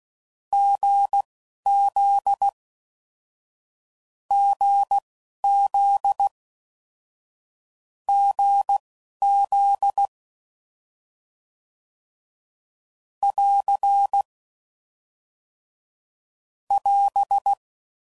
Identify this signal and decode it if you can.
Morse code, heard as a keyed tone, about 16 wpm, spaced as for 11 wpm, 800 Hz. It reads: GZ GZ GZ <AR> <AS>